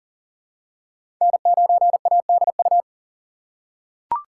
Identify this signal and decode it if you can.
Morse code, heard as a keyed tone, about 40 wpm, 700 Hz. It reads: N9ADU
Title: USA Call Signs - 5 Characters - 4x Character Spacing - 40wpm